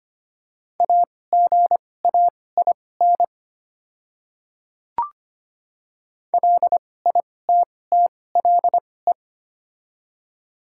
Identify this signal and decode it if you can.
Morse code, heard as a keyed tone, about 25 wpm, spaced as for 18 wpm, 700 Hz. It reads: AGAIN LITTLE
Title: 2x Character Spacing - Top 200 Words - 25wpm